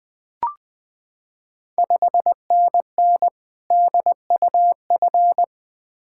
Morse code, heard as a keyed tone, about 21 wpm, 700 Hz. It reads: 5NN DUF
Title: Ontario QSO Party Exchange 20wpm